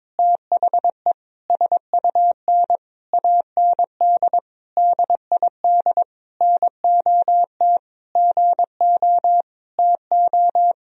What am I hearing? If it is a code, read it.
Morse code, heard as a keyed tone, about 22 wpm, 700 Hz. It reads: THE SUN AND DID NOT GO TO
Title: Sentences from Top 1000 Words - Review Part 2 - 22wpm